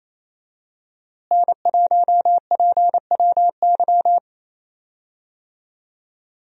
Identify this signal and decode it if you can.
Morse code, heard as a keyed tone, about 27 wpm, 700 Hz. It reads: N1PWY
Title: USA Call Signs - 5 Characters - Repeated 2x - 28wpm